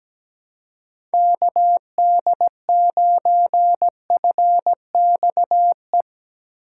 Morse code, heard as a keyed tone, about 17 wpm, 700 Hz. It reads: KD9FXE
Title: USA Call Signs - 4x Character Spacing - 17wpm